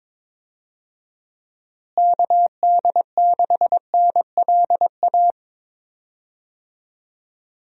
Morse code, heard as a keyed tone, about 22 wpm, 700 Hz. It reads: KD6NLA
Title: USA Call Signs - 6 Characters - Repeated 2x - 22wpm